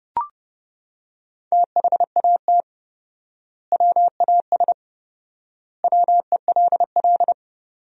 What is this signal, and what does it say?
Morse code, heard as a keyed tone, about 30 wpm, 700 Hz. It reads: THAT WAS WELL
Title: 4x Word-Spacing - Sentences from Top 500 Words - 30wpm